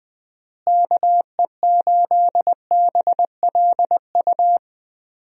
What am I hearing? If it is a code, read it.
Morse code, heard as a keyed tone, about 20 wpm, 700 Hz. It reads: KE8BLU
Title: USA Call Signs - 6 Characters - Repeated 3x - 20wpm